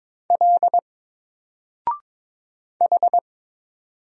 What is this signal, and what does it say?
Morse code, heard as a keyed tone, about 22 wpm, 700 Hz. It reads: L H
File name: Single-Letter-Number-22wpm